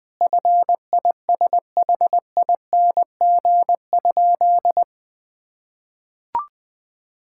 Morse code, heard as a keyed tone, about 20 wpm, 700 Hz. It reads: FISHING?
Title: Sentences from Top 1000 Words - Review Part 5 - 20wpm